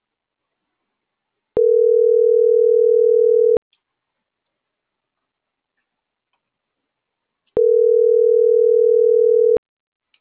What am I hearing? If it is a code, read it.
Ringback tone (North America), 440 + 480 Hz.